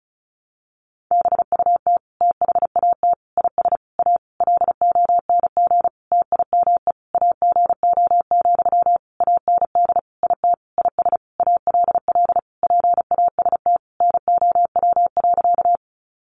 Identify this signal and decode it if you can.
Morse code, heard as a keyed tone, about 34 wpm, 700 Hz. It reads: BUT THAT IS A LONG TIME AGO, AND IT IS ALL PAST NOW.
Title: Sentences from Top 600 Words 35wpm